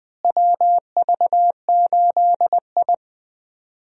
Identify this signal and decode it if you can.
Morse code, heard as a keyed tone, about 20 wpm, 700 Hz. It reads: WV8I
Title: USA Call Signs - 4 Characters - Repeated 3x - 20wpm